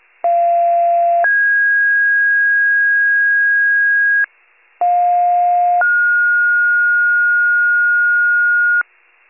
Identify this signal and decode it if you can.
Two-tone page, heard as a one-second tone, then a three-second tone, about 690 then 1650 Hz, about 710 then 1430 Hz, with noise. It is heard from 0.2 to 8.8 s.
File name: pager / 46 FIRE / wcalert